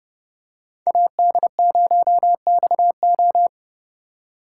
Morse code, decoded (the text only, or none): AD0XO